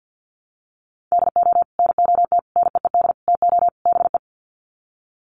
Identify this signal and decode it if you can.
Morse code, heard as a keyed tone, about 50 wpm, 700 Hz. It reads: DO NOT NEED TO BE
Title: Sets of 5 Words Encore 50wpm